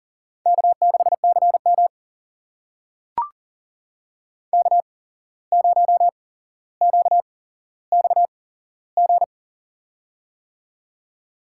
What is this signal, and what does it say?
Morse code, heard as a keyed tone, about 44 wpm, spaced as for 16 wpm, 690 Hz. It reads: K6CM K0QXG
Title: USA Call Signs - 8x Character Spacing - 40wpm